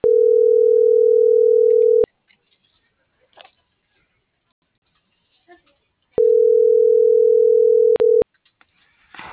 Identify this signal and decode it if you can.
Ringback tone (North America), 440 + 480 Hz.